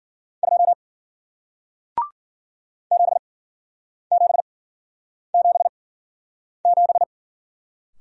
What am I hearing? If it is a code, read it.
Morse code, heard as a keyed tone, about 55 wpm, spaced as for 9 wpm, 700 Hz. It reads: 2 7777